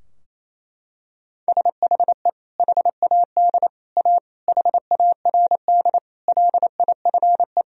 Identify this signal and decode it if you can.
Morse code, heard as a keyed tone, about 28 wpm, 700 Hz. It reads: SHE HAD A HARD LIFE
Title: Sentences from Top 1000 Words - Review Part 2 - 28wpm